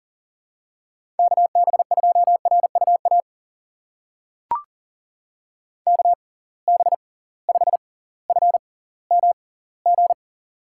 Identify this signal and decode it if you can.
Morse code, heard as a keyed tone, about 40 wpm, spaced as for 20 wpm, 700 Hz. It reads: KB2RUA KB5FMG
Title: USA Call Signs - 6 Characters - 6x Character Spacing - 40wpm